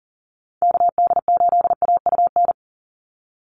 Morse code, heard as a keyed tone, about 40 wpm, 700 Hz. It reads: KD8AUN
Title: USA Call Signs 40wpm